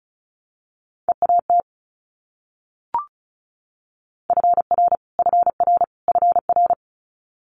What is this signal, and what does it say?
Morse code, heard as a keyed tone, about 36 wpm, 700 Hz. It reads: EAT FR FR FR
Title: Multi-letter Phonemes Repeated 3x 35wpm